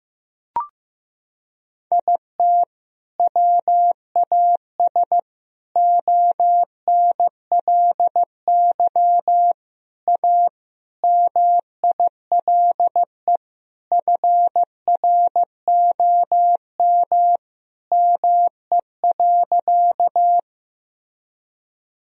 Morse code, heard as a keyed tone, about 15 wpm, 700 Hz. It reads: IT WAS ONLY A MILE FROM ME.
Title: Sentences from Top 1000 Words - Review Part 1 - 15wpm